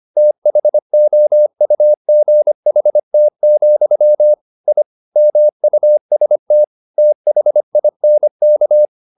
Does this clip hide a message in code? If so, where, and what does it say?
Yes: Morse code from 0.2 to 8.9 s, heard as a keyed tone, about 25 wpm, 600 Hz. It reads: THOUGHT, I MUST THINK